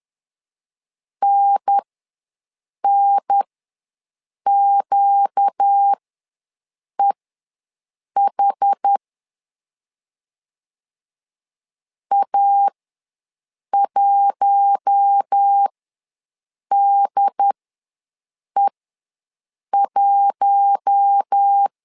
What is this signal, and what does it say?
Morse code, heard as a keyed tone, about 10 wpm, spaced as for 6 wpm, 800 Hz. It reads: NNQEH A1DE1
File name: LEZIONI / 4 Q1 ws 10wpm 2538